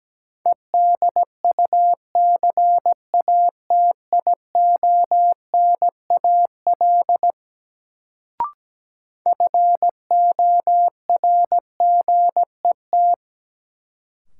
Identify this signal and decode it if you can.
Morse code, heard as a keyed tone, about 17 wpm, 700 Hz. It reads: EDUCATIONAL FORGET
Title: Sentences from Top 1200 Words - New Word Drill - 17wpm